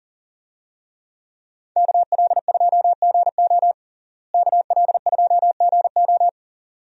Morse code, heard as a keyed tone, about 40 wpm, 700 Hz. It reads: KL2GO KL2GO
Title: USA Call Signs - 5 Characters - Repeated 2x - 40wpm